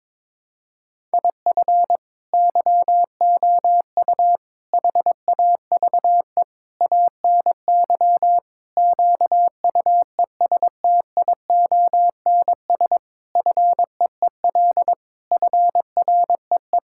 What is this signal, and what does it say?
Morse code, heard as a keyed tone, about 22 wpm, 700 Hz. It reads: IF YOU HAVE ANY QUESTIONS FEEL FREE
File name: practice / 22wpm / Sets of 8 Words - 22wpm